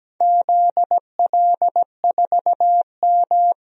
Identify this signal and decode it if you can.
Morse code, heard as a keyed tone, about 17 wpm, 700 Hz. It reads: ZL4M
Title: DXCC-100 Call Signs - 4 Characters - Repeated 2x - 17wpm